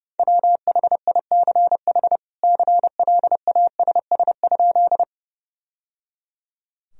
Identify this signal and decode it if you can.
Morse code, heard as a keyed tone, about 31 wpm, 700 Hz. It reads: WHICH CLASS?
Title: Sentences from Top 1000 Words - Review Part 2 - 30wpm